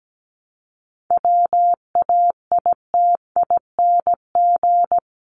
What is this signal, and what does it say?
Morse code, heard as a keyed tone, about 17 wpm, 700 Hz. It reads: WAITING
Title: ING Suffix - Top 100 Words with Base Words 17wpm